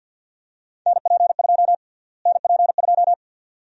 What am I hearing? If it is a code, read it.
Morse code, heard as a keyed tone, about 52 wpm, 700 Hz. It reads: NP2 NP2
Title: USA Call Sign Prefixes - Repeated 2x - 50wpm